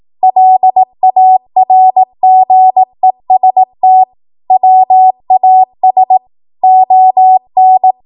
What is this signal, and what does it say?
Morse code, heard as a keyed tone, about 18 wpm, 750 Hz. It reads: LARGEST WAS ON
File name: Morse / 18WPM